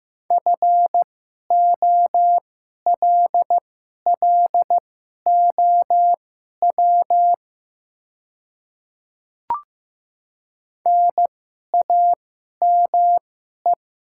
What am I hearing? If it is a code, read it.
Morse code, heard as a keyed tone, about 15 wpm, spaced as for 11 wpm, 700 Hz. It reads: FOLLOW NAME
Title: 2x Character Spacing - Top 300 Words - 15wpm